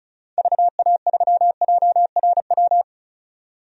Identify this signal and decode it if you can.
Morse code, heard as a keyed tone, about 34 wpm, 700 Hz. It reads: VA3JRW
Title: DXCC-100 Call Signs - 6 Characters - 35wpm